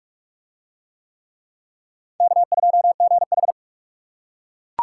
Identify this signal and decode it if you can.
Morse code, heard as a keyed tone, about 41 wpm, 700 Hz. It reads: K2GH